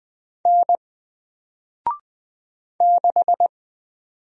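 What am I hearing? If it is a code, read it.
Morse code, heard as a keyed tone, about 20 wpm, 700 Hz. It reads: N 6
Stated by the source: Single letter and number Morse drill at 20 wpm, from N4LCD.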